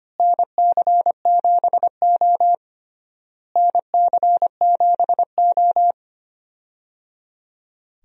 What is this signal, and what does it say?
Morse code, heard as a keyed tone, about 25 wpm, 700 Hz. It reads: NC7O NC7O
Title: USA Call Signs - 4 Characters - Repeated 3x - 25wpm